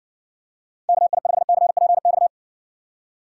Morse code, heard as a keyed tone, about 59 wpm, 700 Hz. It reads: KI5CPX